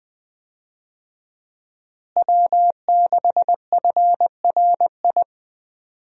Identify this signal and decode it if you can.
Morse code, heard as a keyed tone, about 20 wpm, 700 Hz. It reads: W6FRI
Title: USA Call Signs - 5 Characters - Repeated 3x - 20wpm